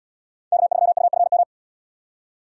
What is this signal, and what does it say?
Morse code, heard as a keyed tone, about 77 wpm, 700 Hz. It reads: K3UFM